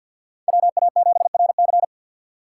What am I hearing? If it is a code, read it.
Morse code, heard as a keyed tone, about 52 wpm, 700 Hz. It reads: WA7RC
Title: USA Call Signs - 5 Characters - 3x Character Spacing - 50wpm